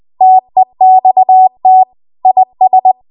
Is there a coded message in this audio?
Yes: Morse code from 0.2 to 2.9 s, heard as a keyed tone, about 20 wpm, 750 Hz. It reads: TEXT IS